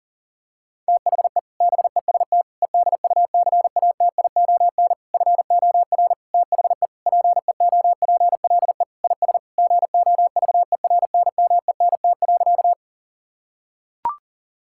Morse code, heard as a keyed tone, about 40 wpm, 700 Hz. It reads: THE BEST EDUCATION FOR THE PEOPLE IS GOVERNMENT.
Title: Sentences from Top 1000 Words - Review Part 6 - 40wpm